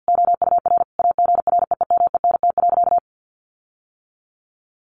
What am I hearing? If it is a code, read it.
Morse code, heard as a keyed tone, about 50 wpm, 700 Hz. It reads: OUR AGREEMENT.
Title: Sentences from Top 1500 Words 50wpm